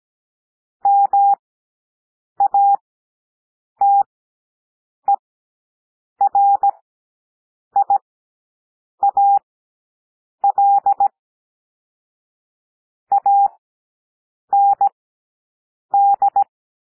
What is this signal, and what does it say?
Morse code, heard as a keyed tone, about 17 wpm, spaced as for 7 wpm, 800 Hz. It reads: MATERIAL AND
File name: SOUND PLAIN 17 wpm